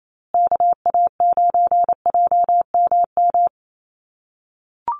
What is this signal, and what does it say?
Morse code, heard as a keyed tone, about 28 wpm, 700 Hz. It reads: KA9JMM
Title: USA Call Signs Repeated 2x 28wpm